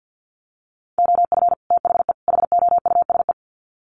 Morse code, heard as a keyed tone, about 50 wpm, 700 Hz. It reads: OF THE HOUSE
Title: Sets of 3 Words 50wpm